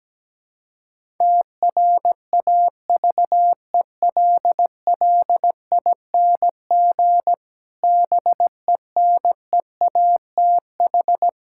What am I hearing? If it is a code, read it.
Morse code, heard as a keyed tone, about 17 wpm, 700 Hz. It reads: TRAVELLING BENEATH
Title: Pangram Challenge v2 - 17wpm